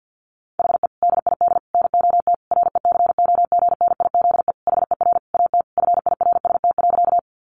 Morse code, heard as a keyed tone, about 50 wpm, 700 Hz. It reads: HE DID NOT RECOGNIZE HER AT FIRST.
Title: Sentences from Top 1400 Words 50wpm